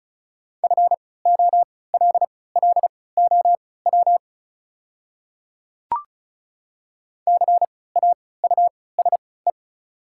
Morse code, heard as a keyed tone, about 36 wpm, spaced as for 20 wpm, 700 Hz. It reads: FOLLOW CAUSE
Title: 3x Character Spacing - Top 300 Words - 35wpm